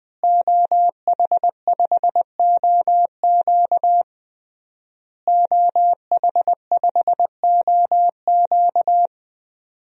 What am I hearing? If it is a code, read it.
Morse code, heard as a keyed tone, about 20 wpm, 700 Hz. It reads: OH5OQ OH5OQ
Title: DXCC-100 Call Signs - 5 Characters - Repeated 3x - 20wpm